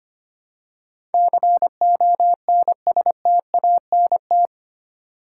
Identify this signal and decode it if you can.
Morse code, heard as a keyed tone, about 25 wpm, 700 Hz. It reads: CONSTANT
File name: 8 Letter Words - First 3 Letters Given - 25wpm